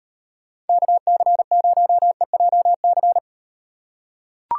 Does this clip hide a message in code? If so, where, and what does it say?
Yes: Morse code from 0.7 to 3.2 s, heard as a keyed tone, about 38 wpm, 700 Hz. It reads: KC0EJC